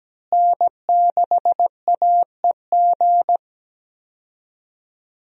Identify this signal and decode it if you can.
Morse code, heard as a keyed tone, about 17 wpm, 700 Hz. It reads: N6AEG
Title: USA Call Signs - 5 Characters - Repeated 2x - 17wpm